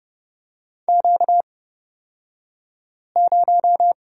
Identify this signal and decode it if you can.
Morse code, heard as a keyed tone, about 30 wpm, spaced as for 5 wpm, 700 Hz. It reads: Q0
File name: Lesson 38d - Letter q - Two Characters 30wpm